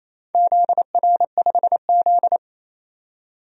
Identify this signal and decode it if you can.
Morse code, heard as a keyed tone, about 28 wpm, 700 Hz. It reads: ZR5Z
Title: DXCC-100 Call Signs - 4 Characters - 28wpm